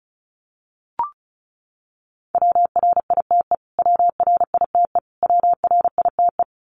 Morse code, heard as a keyed tone, about 34 wpm, 700 Hz. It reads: WRITE WRITE WRITE
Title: Top 200 Words Repeated 35wpm